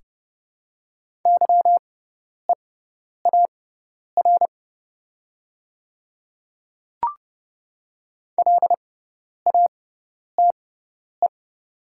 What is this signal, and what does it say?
Morse code, heard as a keyed tone, about 30 wpm, spaced as for 10 wpm, 700 Hz. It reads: YEAR LATE
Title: 6x Character Spacing - Top 500 Words - 30wpm